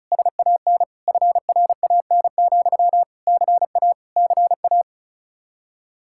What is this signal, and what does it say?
Morse code, heard as a keyed tone, about 44 wpm, 700 Hz. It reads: SAN FRAN, CA CA